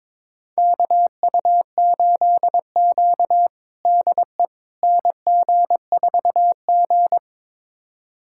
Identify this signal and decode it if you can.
Morse code, heard as a keyed tone, about 22 wpm, 700 Hz. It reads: KU8Q DE NG4G
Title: USA Callsign DE Callsign - 4 Characters - 22wpm